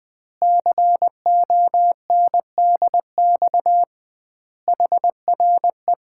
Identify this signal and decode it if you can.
Morse code, heard as a keyed tone, about 21 wpm, 700 Hz. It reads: CONDX HRE
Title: Common QSO Phrases - 2x Word-Spacing - 20wpm